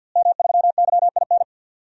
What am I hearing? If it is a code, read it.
Morse code, heard as a keyed tone, about 52 wpm, 700 Hz. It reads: M3YIN
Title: DXCC-100 Call Signs - 5 Characters - Repeated 2x - 50wpm